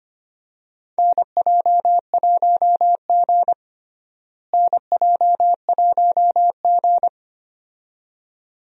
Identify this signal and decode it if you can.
Morse code, heard as a keyed tone, about 25 wpm, 700 Hz. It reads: NJ1G NJ1G
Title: USA Call Signs - 4 Characters - Repeated 2x - 25wpm